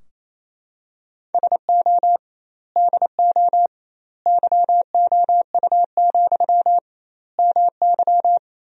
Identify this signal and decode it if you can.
Morse code, heard as a keyed tone, about 28 wpm, 700 Hz. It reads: SO DO YOU, MY
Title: 2x Word-Spacing - Sentences from Top 400 Words - 28wpm